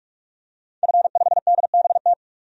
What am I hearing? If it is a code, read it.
Morse code, heard as a keyed tone, about 46 wpm, 700 Hz. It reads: F5DBT
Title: DXCC-100 Call Signs - 5 Characters - 45wpm